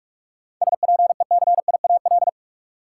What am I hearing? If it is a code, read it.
Morse code, heard as a keyed tone, about 46 wpm, 700 Hz. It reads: SPECIAL
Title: Sentences from Top 1200 Words - New Word Drill Alt - 45wpm